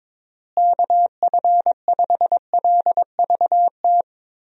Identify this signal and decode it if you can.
Morse code, heard as a keyed tone, about 22 wpm, 700 Hz. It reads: KF5LVT
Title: USA Call Signs - 6 Characters - Repeated 3x - 22wpm